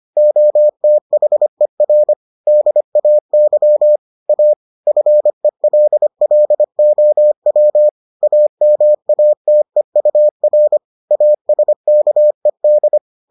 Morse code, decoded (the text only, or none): OTHER DAY A FELLOW AMATEUR ASKED